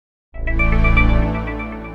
There is no code.